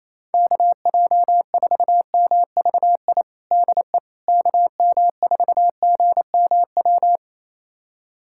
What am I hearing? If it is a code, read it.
Morse code, heard as a keyed tone, about 28 wpm, 700 Hz. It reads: KJ4MVI DE KM4GMW
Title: USA Callsign DE Callsign - 6 Characters - 28wpm